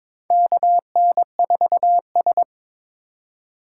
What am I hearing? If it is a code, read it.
Morse code, heard as a keyed tone, about 22 wpm, 700 Hz. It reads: KN4S
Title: USA Call Signs - 4 Characters - 3x Character Spacing - 22wpm